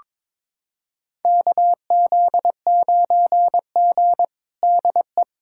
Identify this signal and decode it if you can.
Morse code, heard as a keyed tone, about 22 wpm, 700 Hz. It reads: KZ9G DE
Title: USA Callsign DE Callsign - 4 Characters - 22wpm